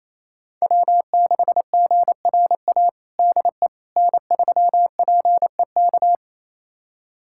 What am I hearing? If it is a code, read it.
Morse code, heard as a keyed tone, about 28 wpm, 700 Hz. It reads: W6GRA DE N3PEK
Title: USA Callsign DE Callsign 28wpm